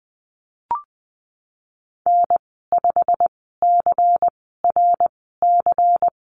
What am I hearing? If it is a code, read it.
Morse code, heard as a keyed tone, about 20 wpm, spaced as for 14 wpm, 700 Hz. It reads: N5CRC
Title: 179 Lesson 29g - Slash - Call Signs 20wpm